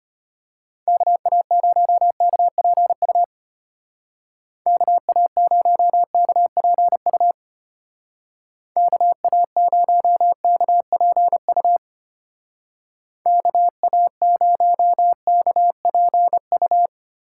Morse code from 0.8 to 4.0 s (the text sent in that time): KA0KPU